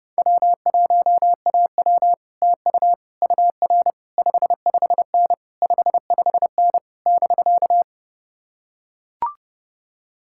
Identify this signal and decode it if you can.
Morse code, heard as a keyed tone, about 30 wpm, 700 Hz. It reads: W1AW TU UR 55N 55N <BK>